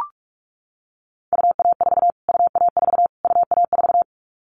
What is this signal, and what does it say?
Morse code, heard as a keyed tone, about 44 wpm, 700 Hz. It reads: UA4 UA4 UA4
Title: Callsign Prefixes for Top 25 DXCC - Repeated 3x - 45wpm